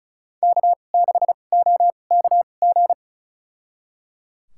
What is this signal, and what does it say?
Morse code, heard as a keyed tone, about 36 wpm, spaced as for 25 wpm, 700 Hz. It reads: K6OKG
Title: USA Call Signs - 2x Character Spacing - 35wpm